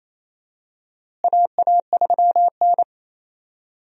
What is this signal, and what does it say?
Morse code, heard as a keyed tone, about 27 wpm, 700 Hz. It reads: AA3N
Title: USA Call Signs - 4 Characters - Repeated 2x - 28wpm